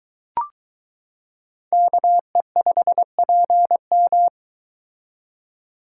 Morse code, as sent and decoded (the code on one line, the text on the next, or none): -.- . ..... .--. --
KE5PM